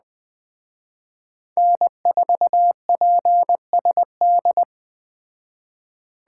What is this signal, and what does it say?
Morse code, heard as a keyed tone, about 20 wpm, 700 Hz. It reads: N4PSD